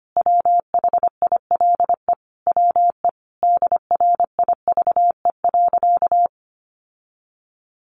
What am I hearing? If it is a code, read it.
Morse code, heard as a keyed tone, about 25 wpm, 700 Hz. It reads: WHILE WE DRIVE.
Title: Sentences from Top 1300 Words 25wpm